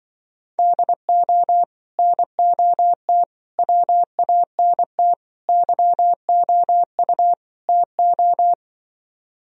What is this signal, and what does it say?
Morse code, heard as a keyed tone, about 24 wpm, 700 Hz. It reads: DO NOT WANT YOU TO
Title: Speed-Racing - Sets of 5 Words 20wpm